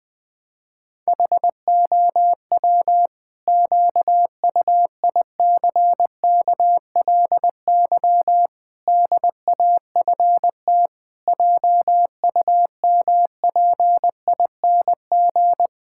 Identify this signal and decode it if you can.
Morse code, heard as a keyed tone, about 20 wpm, 700 Hz. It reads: HOW QUICKLY DAFT JUMPING
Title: Pangram Challenge v2 - 20wpm